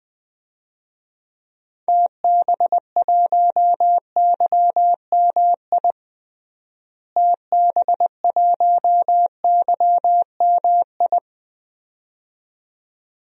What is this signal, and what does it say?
Morse code, heard as a keyed tone, about 21 wpm, 700 Hz. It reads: TB1YMI TB1YMI
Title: DXCC-100 Call Signs - 6 Characters - Repeated 2x - 20wpm